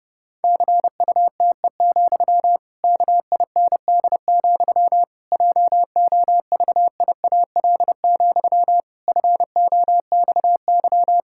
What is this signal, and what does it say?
Morse code, heard as a keyed tone, about 30 wpm, 700 Hz. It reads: CUTE, KIND, JOVIAL, FOXY
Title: Pangram Challenge v2 - 30wpm